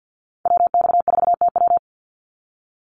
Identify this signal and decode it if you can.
Morse code, heard as a keyed tone, about 48 wpm, 700 Hz. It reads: WX4TW
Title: NA SOTA Activator Callsigns 50wpm